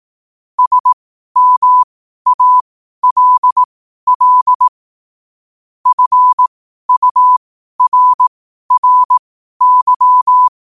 Morse code, heard as a keyed tone, about 18 wpm, spaced as for 12 wpm, 1000 Hz. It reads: SMALL FURRY
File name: qotdfiles / 12wpm